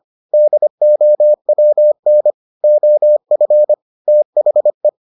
Morse code, heard as a keyed tone, about 25 wpm, 600 Hz. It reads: DOWN OF THE